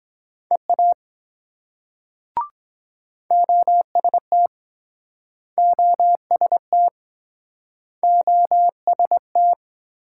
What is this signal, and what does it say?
Morse code, heard as a keyed tone, about 22 wpm, 700 Hz. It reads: EA OST OST OST